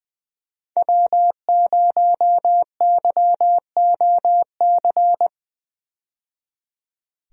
Morse code, heard as a keyed tone, about 20 wpm, 700 Hz. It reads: W0YOC
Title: USA Call Signs - 5 Characters - 20wpm